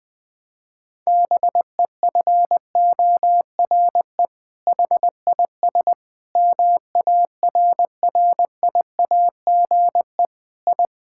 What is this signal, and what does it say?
Morse code, heard as a keyed tone, about 20 wpm, 700 Hz. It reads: BEFORE HIS MARRIAGE I
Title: Sentences from Top 1000 Words - Review Part 5 - 20wpm